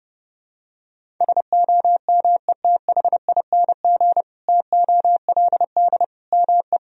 Morse code, heard as a keyed tone, about 32 wpm, 700 Hz. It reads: SOMETHING TOLD ME
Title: Sentences from Top 1000 Words - Review Part 1 - 30wpm